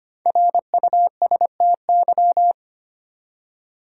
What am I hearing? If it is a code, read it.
Morse code, heard as a keyed tone, about 25 wpm, 700 Hz. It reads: RUSTY
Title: USA Common Names - 25wpm